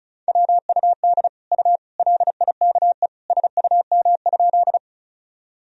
Morse code, heard as a keyed tone, about 34 wpm, 700 Hz. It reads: WUD U LIKE SUM?
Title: Sentences from Top 100 Words and Abbreviations 35wpm